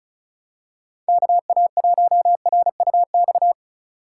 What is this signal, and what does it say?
Morse code, heard as a keyed tone, about 34 wpm, 700 Hz. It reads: KA1RUX
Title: USA Call Signs - 2x Character Spacing - 35wpm